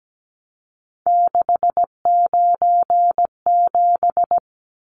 Morse code, heard as a keyed tone, about 17 wpm, 700 Hz. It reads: 697
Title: Numbers - 3-Digit 17wpm